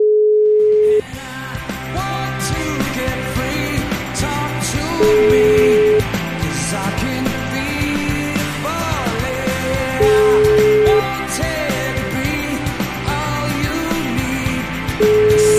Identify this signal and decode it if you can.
Ringback tone (Europe, 425 Hz), with noise.